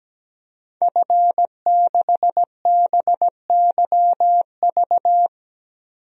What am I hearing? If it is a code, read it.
Morse code, heard as a keyed tone, about 17 wpm, 700 Hz. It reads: F6BYV
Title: DXCC-100 Call Signs - 5 Characters - Repeated 3x - 17wpm